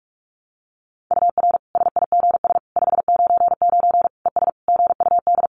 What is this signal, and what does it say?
Morse code, heard as a keyed tone, about 44 wpm, 700 Hz. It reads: UR SIGS 599 ES GUD
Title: Common QSO Phrases 45wpm